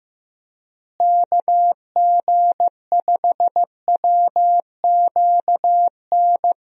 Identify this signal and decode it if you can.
Morse code, heard as a keyed tone, about 15 wpm, 700 Hz. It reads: KG5WQN
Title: USA Call Signs - 6 Characters - Repeated 3x - 15wpm